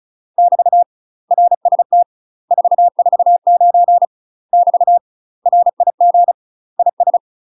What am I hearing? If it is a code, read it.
Morse code, heard as a keyed tone, about 34 wpm, 700 Hz. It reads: <BT> RST 449 <BT> RIG IS
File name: UK MASTER QSOs TRAINING / 35WPM / UK MASTER QSOs 0005